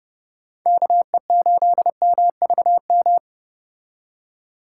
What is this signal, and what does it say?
Morse code, heard as a keyed tone, about 29 wpm, 700 Hz. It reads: KE8MVM